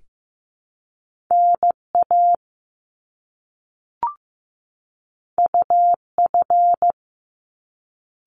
Morse code, heard as a keyed tone, about 15 wpm, 700 Hz. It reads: NA UF
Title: Top Two Letter Combinations in Words 15wpm